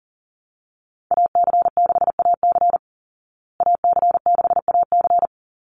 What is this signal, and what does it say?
Morse code, heard as a keyed tone, about 40 wpm, 700 Hz. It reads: AC6AC AC6AC
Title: Call Signs Repeated 3x - 40wpm